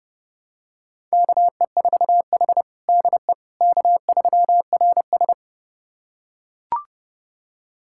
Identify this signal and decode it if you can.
Morse code, heard as a keyed tone, about 30 wpm, 700 Hz. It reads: KE4H DE K3RS
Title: USA Callsign DE Callsign - 4 Characters - 30wpm